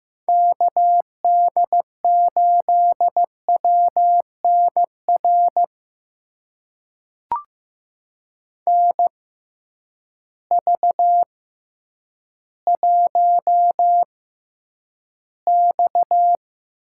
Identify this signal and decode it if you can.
Morse code, heard as a keyed tone, about 15 wpm, spaced as for 9 wpm, 700 Hz. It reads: KD8WNR NV1X
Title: USA Call Signs - 6x Character Spacing - 15wpm